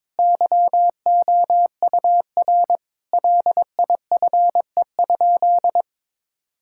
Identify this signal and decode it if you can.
Morse code, heard as a keyed tone, about 22 wpm, 700 Hz. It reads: YOUR LIFE?
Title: Sentences from Top 1000 Words - Review Part 6 - 22wpm